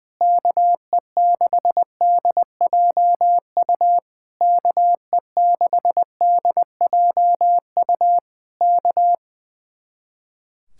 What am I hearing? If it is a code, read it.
Morse code, heard as a keyed tone, about 20 wpm, 700 Hz. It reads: KE6DJU KE6DJU K
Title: Speed-Racing Common QSO Phrases - 20wpm